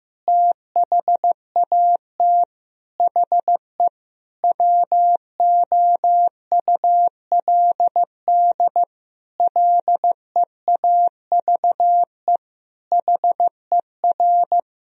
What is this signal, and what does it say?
Morse code, heard as a keyed tone, about 15 wpm, 700 Hz. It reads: THAT HE WOULD LEAVE HER
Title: Sentences from Top 1000 Words - Review Part 4 - 15wpm